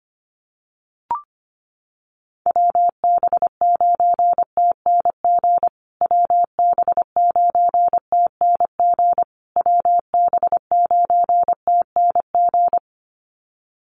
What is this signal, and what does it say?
Morse code, heard as a keyed tone, about 25 wpm, 700 Hz. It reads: WB9TNG WB9TNG WB9TNG
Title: USA Call Signs Repeated 3x 25wpm